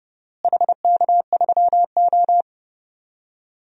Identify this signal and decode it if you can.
Morse code, heard as a keyed tone, about 30 wpm, 700 Hz. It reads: HK3O
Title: DXCC-100 Call Signs - 4 Characters - Repeated 2x - 30wpm